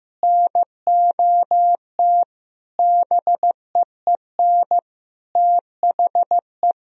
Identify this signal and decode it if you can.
Morse code, heard as a keyed tone, about 15 wpm, 700 Hz. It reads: NOT BEEN THE
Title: Sentences from Top 1000 Words - Review Part 1 - 15wpm